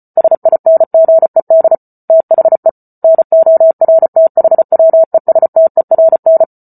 Morse code, heard as a keyed tone, about 34 wpm, 650 Hz. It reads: SINGED THE NORTHWESTERN